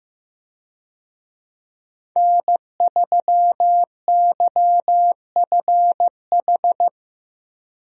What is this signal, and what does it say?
Morse code, heard as a keyed tone, about 15 wpm, 700 Hz. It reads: N3YFH
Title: USA Call Signs - 5 Characters - 15wpm